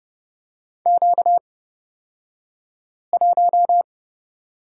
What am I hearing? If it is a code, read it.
Morse code, heard as a keyed tone, about 30 wpm, spaced as for 5 wpm, 700 Hz. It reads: Q1
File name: Lesson 38d - Letter q - Two Characters 30wpm